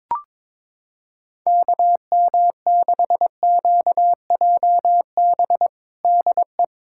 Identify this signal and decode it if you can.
Morse code, heard as a keyed tone, about 22 wpm, 700 Hz. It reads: KM6QJB DE
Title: USA Callsign DE Callsign - 6 Characters - 22wpm